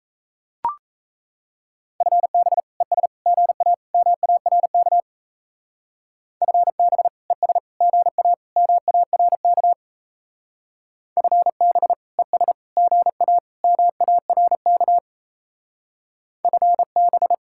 Morse code, heard as a keyed tone, about 36 wpm, 700 Hz. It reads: FB ES GA MARK FB ES GA MARK FB ES GA MARK FB